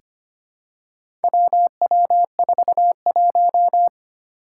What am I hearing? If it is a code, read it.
Morse code, heard as a keyed tone, about 25 wpm, 700 Hz. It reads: WW41
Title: Four Challenging Letter-Number Combinations 25wpm